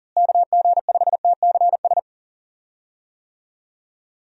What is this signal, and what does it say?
Morse code, heard as a keyed tone, about 40 wpm, 700 Hz. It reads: KG5TCS
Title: USA Call Signs - 6 Characters - Repeated 2x - 40wpm